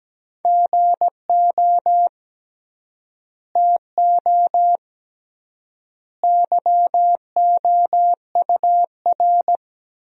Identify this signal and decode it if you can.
Morse code, heard as a keyed tone, about 17 wpm, 700 Hz. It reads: GO TO YOUR
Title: 3x Word-Spacing - Sentences from Top 300 Words - 17wpm